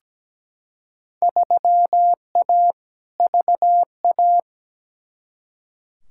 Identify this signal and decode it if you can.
Morse code, heard as a keyed tone, about 17 wpm, 700 Hz. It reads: 3A VA
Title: Field Day Exchanges - 17wpm